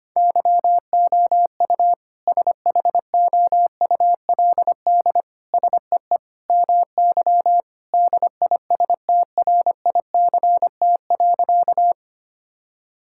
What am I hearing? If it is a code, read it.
Morse code, heard as a keyed tone, about 25 wpm, 700 Hz. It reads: YOU SHOULD SEE MY DISTRICT.
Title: Sentences from Top 1600 Words 25wpm